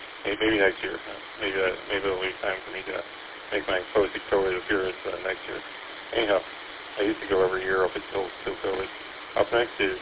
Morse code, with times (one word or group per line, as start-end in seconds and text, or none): none